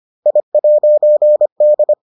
Morse code, heard as a keyed tone, about 25 wpm, 600 Hz. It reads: I'D